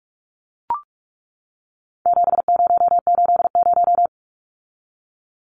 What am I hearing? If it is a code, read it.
Morse code, heard as a keyed tone, about 44 wpm, 700 Hz. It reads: 7080
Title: Numbers - 4-Digit 45wpm